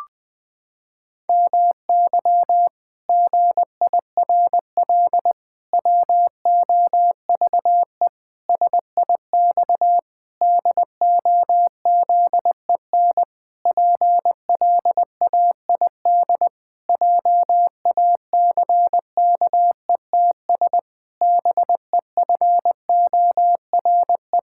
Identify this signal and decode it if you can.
Morse code, heard as a keyed tone, about 19 wpm, 700 Hz. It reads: MY GIRL WOVE SIX DOZEN PLAID JACKETS BEFORE
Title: Pangram Challenge v2 - 20wpm